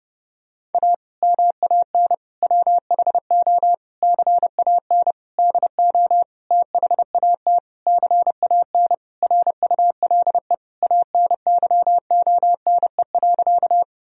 Morse code, heard as a keyed tone, about 30 wpm, 700 Hz. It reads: A MAN WHO CAN DO THAT CAN RULE ANYONE.
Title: Sentences from Top 1000 Words - Review Part 2 - 30wpm